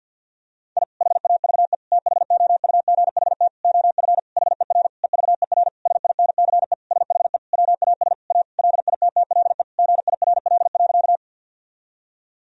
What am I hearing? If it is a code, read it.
Morse code, heard as a keyed tone, about 50 wpm, 700 Hz. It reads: I HAVE THOUGHT OF HER EVER SINCE SHE WAS A LITTLE GIRL.
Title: Sentences from Top 1000 Words - Review Part 1 - 50wpm